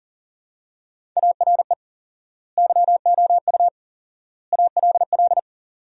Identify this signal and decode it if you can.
Morse code, heard as a keyed tone, about 37 wpm, 700 Hz. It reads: ARE YOU ALL
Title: 4x Word-Spacing - Sentences from Top 400 Words - 40wpm